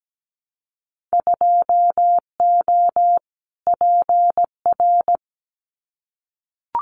Morse code, heard as a keyed tone, about 17 wpm, 700 Hz. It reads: 2O PR
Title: Winter Field Day Exchanges Encore 17wpm